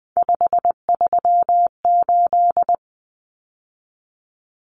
Morse code, heard as a keyed tone, about 20 wpm, 700 Hz. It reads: 538
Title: Numbers - 3-Digit 20wpm